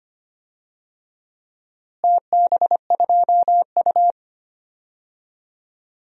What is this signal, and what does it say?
Morse code, heard as a keyed tone, about 25 wpm, 700 Hz. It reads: TB2U
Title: DXCC-100 Call Signs - 4 Characters - Repeated 2x - 25wpm